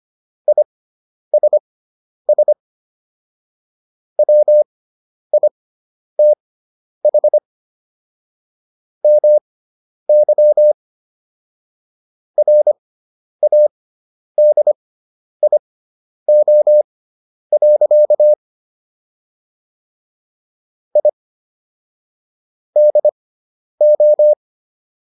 Morse code, heard as a keyed tone, about 25 wpm, spaced as for 10 wpm, 600 Hz. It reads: ISS WITH MY RADIO. I DO